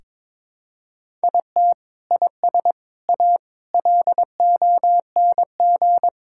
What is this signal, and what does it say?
Morse code, heard as a keyed tone, about 22 wpm, 700 Hz. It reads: IT IS A LONG
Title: Sentences from Top 1000 Words - Review Part 5 - 22wpm